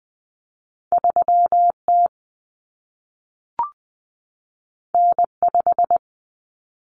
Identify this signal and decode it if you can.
Morse code, heard as a keyed tone, about 20 wpm, 700 Hz. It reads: 3T N5
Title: Common Cut Numbers - 2-Digit 20wpm